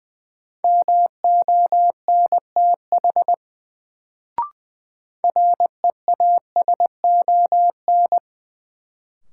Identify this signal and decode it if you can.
Morse code, heard as a keyed tone, about 21 wpm, 700 Hz. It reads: MONTH REASON
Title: Sets of 5 Words Encore - New Word Drill - 20wpm